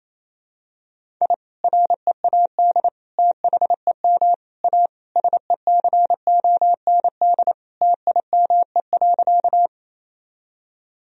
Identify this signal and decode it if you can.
Morse code, heard as a keyed tone, about 28 wpm, 700 Hz. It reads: I READ THEM A SECOND TIME.
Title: Sentences from Top 1000 Words - Review Part 1 - 28wpm